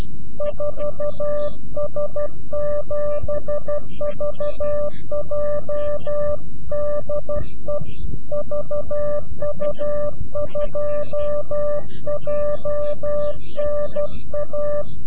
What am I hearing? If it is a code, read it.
Morse code, heard as a keyed tone, about 12 wpm, 590 Hz. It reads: 4S7VJ DE VU2JNA
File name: CW-QSO / CQ Call / CQ-QSO